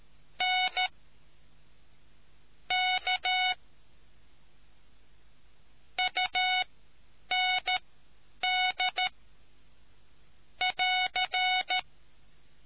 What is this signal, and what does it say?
Morse code, heard as a keyed tone, about 13 wpm, spaced as for 8 wpm, 740 Hz. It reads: N K UND <AR>